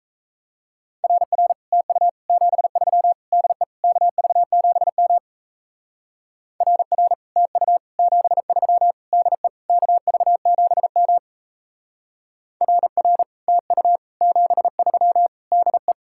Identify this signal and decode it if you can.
Morse code, heard as a keyed tone, about 35 wpm, 700 Hz. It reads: RR TU 73 DE KV7M RR TU 73 DE KV7M RR TU 73 DE
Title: Speed-Racing Common QSO Phrases - 28wpm